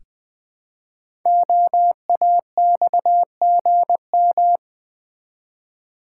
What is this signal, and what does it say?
Morse code, heard as a keyed tone, about 20 wpm, 700 Hz. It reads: OAXGM
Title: Code Groups - 5 Letters - 20wpm